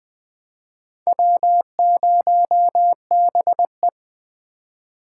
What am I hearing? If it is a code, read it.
Morse code, heard as a keyed tone, about 20 wpm, 700 Hz. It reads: W0BE
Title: USA Call Signs - 3x Character Spacing - 20wpm